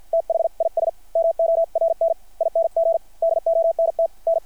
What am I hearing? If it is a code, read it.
Morse code, heard as a keyed tone, about 50 wpm, spaced as for 42 wpm, 660 Hz. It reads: THIS MORNING DONT N
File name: archived / CRSnet